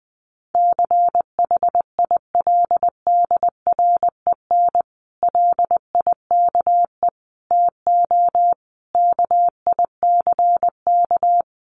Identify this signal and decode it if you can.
Morse code, heard as a keyed tone, about 20 wpm, 700 Hz. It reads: CHILDREN LIKE TO KICK
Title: Sentences from Top 1300 Words 20wpm